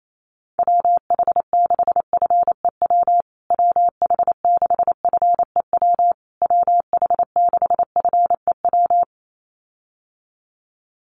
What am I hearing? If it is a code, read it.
Morse code, heard as a keyed tone, about 28 wpm, 700 Hz. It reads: WH6FEW WH6FEW WH6FEW
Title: USA Call Signs Repeated 3x 28wpm